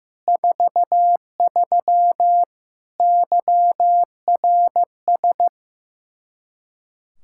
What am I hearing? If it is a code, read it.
Morse code, heard as a keyed tone, about 15 wpm, 700 Hz. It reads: 43 YRS
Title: K1USN SST QSO Practice 15wpm